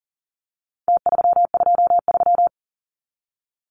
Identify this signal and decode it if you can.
Morse code, heard as a keyed tone, about 40 wpm, 700 Hz. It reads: T323
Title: Numbers - Contest Style - 4-Digit 40wpm